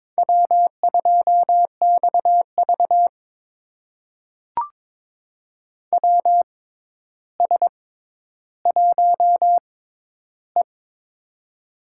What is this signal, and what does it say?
Morse code, heard as a keyed tone, about 22 wpm, spaced as for 11 wpm, 700 Hz. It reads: W2XV WS1E
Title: USA Call Signs - 4 Characters - 6x Character Spacing - 22wpm